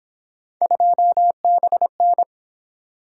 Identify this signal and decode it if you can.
Morse code, heard as a keyed tone, about 28 wpm, 700 Hz. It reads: 2BN